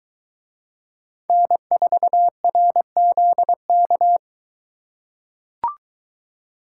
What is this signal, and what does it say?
Morse code, heard as a keyed tone, about 23 wpm, 700 Hz. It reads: N4RZK